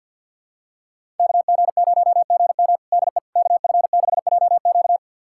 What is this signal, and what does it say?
Morse code, heard as a keyed tone, about 50 wpm, 700 Hz. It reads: KG0GM DE KF6JQ
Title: USA Callsign DE Callsign - 5 Characters - 50wpm